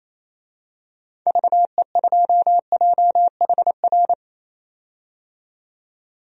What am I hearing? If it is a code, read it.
Morse code, heard as a keyed tone, about 28 wpm, 700 Hz. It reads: VE2JHR
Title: DXCC-100 Call Signs - 6 Characters - 28wpm